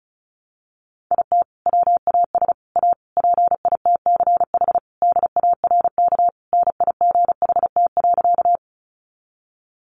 Morse code, heard as a keyed tone, about 34 wpm, 700 Hz. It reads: IT WAS A PITCH DARK NIGHT.
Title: Sentences from Top 1300 Words 35wpm